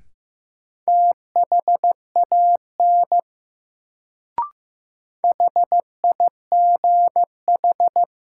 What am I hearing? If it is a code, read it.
Morse code, heard as a keyed tone, about 15 wpm, 700 Hz. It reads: THAN HIGH
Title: Sets of 2 Words - New Word Drill - 15wpm